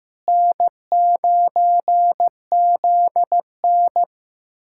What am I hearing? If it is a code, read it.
Morse code, heard as a keyed tone, about 16 wpm, 700 Hz. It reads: N9ZN
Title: USA Callsign DE Callsign - 4 Characters - 15wpm